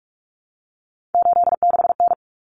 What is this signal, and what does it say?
Morse code, heard as a keyed tone, about 44 wpm, 700 Hz. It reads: 86N
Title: Common Cut Numbers - 3-Digit 45wpm